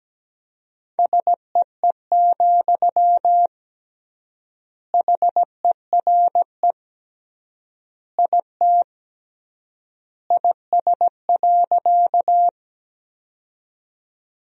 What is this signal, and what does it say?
Morse code, heard as a keyed tone, about 17 wpm, 700 Hz. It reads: SEE, HERE IT IS.
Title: 3x Word-Spacing - Sentences from Top 200 Words - 17wpm